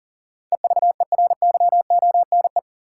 Morse code, heard as a keyed tone, about 40 wpm, 700 Hz. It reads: EVERYONE
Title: Sentences from Top 800 Words - New Word Drill - 40wpm